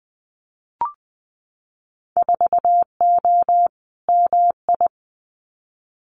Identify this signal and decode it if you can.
Morse code, heard as a keyed tone, about 20 wpm, 700 Hz. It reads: 4O MI
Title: Winter Field Day Exchanges Encore 20wpm